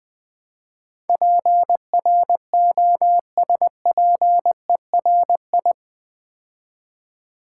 Morse code, heard as a keyed tone, about 20 wpm, 700 Hz. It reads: PROSPERI
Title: Build A Word - 4 Letters At A Time - 10-Letter Words - 20wpm